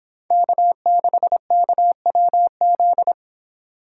Morse code, heard as a keyed tone, about 26 wpm, 700 Hz. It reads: K6KWZ